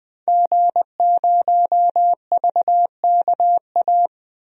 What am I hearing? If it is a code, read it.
Morse code, heard as a keyed tone, about 20 wpm, 700 Hz. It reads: G0VKA
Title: DXCC-100 Call Signs - 5 Characters - 20wpm